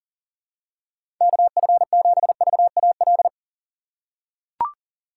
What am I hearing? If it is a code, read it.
Morse code, heard as a keyed tone, about 40 wpm, 700 Hz. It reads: KF7VAL
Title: USA Call Signs - 6 Characters - 6x Character Spacing - 40wpm